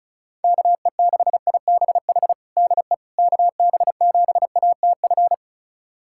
Morse code, heard as a keyed tone, about 36 wpm, 700 Hz. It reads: KE6IBH DE KB7ATF
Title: USA Callsign DE Callsign - 6 Characters - 35wpm